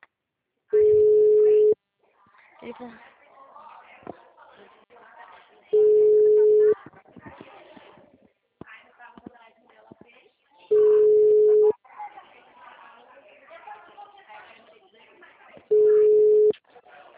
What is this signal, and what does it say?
Ringback tone (Europe, 425 Hz), with speech.